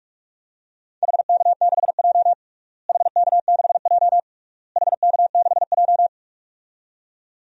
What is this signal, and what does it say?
Morse code, heard as a keyed tone, about 41 wpm, 700 Hz. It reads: HK6J HK6J HK6J
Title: DXCC-100 Call Signs - 4 Characters - Repeated 3x - 45wpm